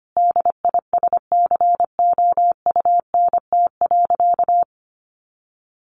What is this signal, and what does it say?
Morse code, heard as a keyed tone, about 25 wpm, 700 Hz. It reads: DISCOUNT.
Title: Sentences from Top 1200 Words 25wpm